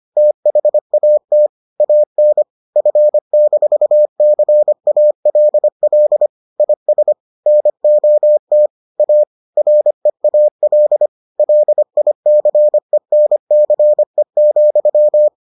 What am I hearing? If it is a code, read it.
Morse code, heard as a keyed tone, about 25 wpm, 600 Hz. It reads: THAT AN F-CALL IS NOT A REAL LICENCE,